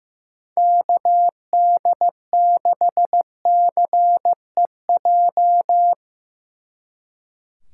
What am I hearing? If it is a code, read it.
Morse code, heard as a keyed tone, about 17 wpm, 700 Hz. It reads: KD6CEJ
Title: Speed Racing - USA 2x3 Call Signs 15wpm